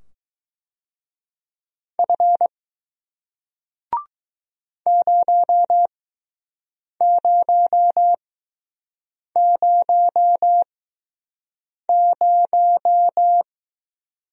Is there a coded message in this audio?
Yes: Morse code from 2.0 to 13.4 s, heard as a keyed tone, about 20 wpm, spaced as for 6 wpm, 700 Hz. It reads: F 0000